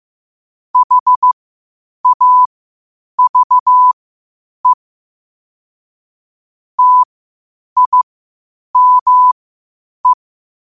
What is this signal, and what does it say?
Morse code, heard as a keyed tone, about 15 wpm, spaced as for 8 wpm, 1000 Hz. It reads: HAVE TIME
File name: qotdfiles / 8wpm